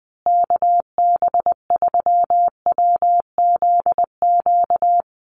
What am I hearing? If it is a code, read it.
Morse code, heard as a keyed tone, about 20 wpm, 700 Hz. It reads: KB3WZQ
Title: USA Call Signs Repeated 3x 20wpm